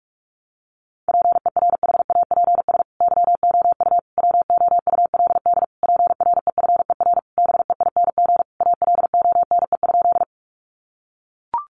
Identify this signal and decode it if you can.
Morse code, heard as a keyed tone, about 44 wpm, 700 Hz. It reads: PERHAPS YOU WOULD PREFER BEING ALONE?
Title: Sentences from Top 1100 Words 45wpm